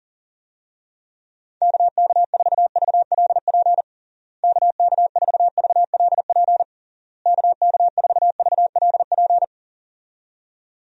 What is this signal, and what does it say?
Morse code, heard as a keyed tone, about 40 wpm, 700 Hz. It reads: KK4VLP KK4VLP KK4VLP
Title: USA Call Signs - 6 Characters - Repeated 3x - 40wpm